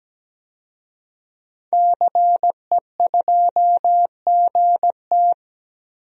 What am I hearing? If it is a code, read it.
Morse code, heard as a keyed tone, about 17 wpm, 700 Hz. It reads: CE2GT
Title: DXCC-100 Call Signs - 5 Characters - 17wpm